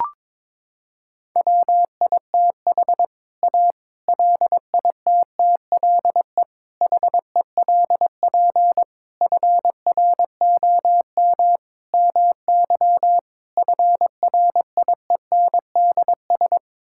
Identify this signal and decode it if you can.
Morse code, heard as a keyed tone, about 22 wpm, 700 Hz. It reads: WITH A LITTLE HELP FROM MY FRIENDS
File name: practice / 22wpm / Sets of 7 Words - 22wpm